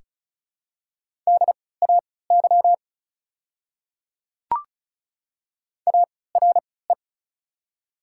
Morse code, heard as a keyed tone, about 36 wpm, spaced as for 20 wpm, 690 Hz. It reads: DAY ARE
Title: 3x Character Spacing - Top 100 Words - 35wpm